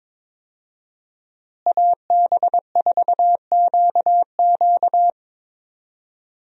Morse code, heard as a keyed tone, about 22 wpm, 700 Hz. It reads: AB4QQ
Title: USA Call Signs - 5 Characters - Repeated 2x - 22wpm